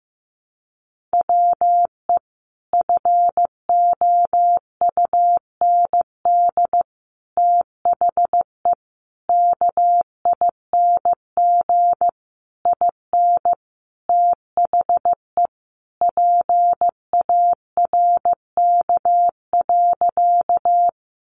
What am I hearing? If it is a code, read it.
Morse code, heard as a keyed tone, about 15 wpm, 700 Hz. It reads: WE FOUND THE KING IN THE PARK.
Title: Sentences from Top 1300 Words 15wpm